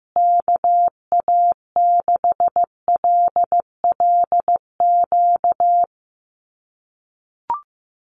Morse code, heard as a keyed tone, about 14 wpm, 700 Hz. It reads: KA6LLQ
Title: USA Call Signs Repeated 3x 15wpm